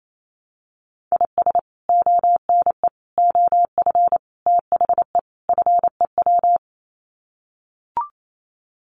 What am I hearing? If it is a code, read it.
Morse code, heard as a keyed tone, about 28 wpm, 700 Hz. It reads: IS ONE OF THE FEW
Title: Sets of 5 Words 28wpm